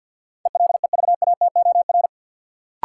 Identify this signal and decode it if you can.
Morse code, heard as a keyed tone, about 49 wpm, 700 Hz. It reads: ELEVATOR
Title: Sentences from Top 1600 Words - New Word Drill - 50wpm